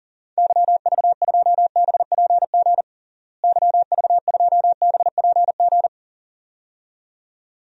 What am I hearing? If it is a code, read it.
Morse code, heard as a keyed tone, about 40 wpm, 700 Hz. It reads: YV2BPG YV2BPG
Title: DXCC-100 Call Signs - 6 Characters - Repeated 2x - 40wpm